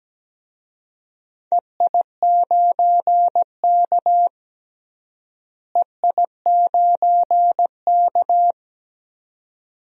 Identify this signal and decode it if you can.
Morse code, heard as a keyed tone, about 17 wpm, 700 Hz. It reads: EI9K EI9K
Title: DXCC-100 Call Signs - 4 Characters - Repeated 2x - 17wpm